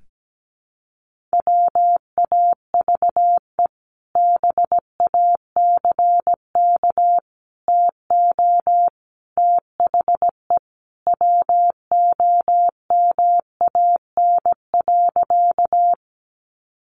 Morse code, heard as a keyed tone, about 17 wpm, 700 Hz. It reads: WAVE BACK TO THE WOMAN.
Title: Sentences from Top 1400 Words 17wpm